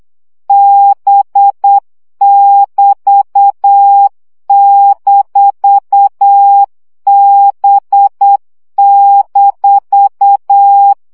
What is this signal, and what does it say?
Morse code, heard as a keyed tone, about 8 wpm, 800 Hz. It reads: B<BT>-B-
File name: CW gyakorlas b 6 egyenlo minusz 7 baud